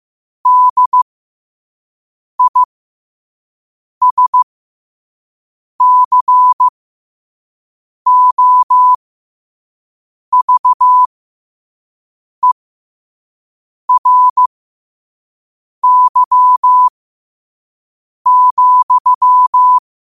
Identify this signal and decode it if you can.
Morse code, heard as a keyed tone, about 15 wpm, spaced as for 5 wpm, 1000 Hz. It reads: DISCOVERY,
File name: qotdfiles / 5wpm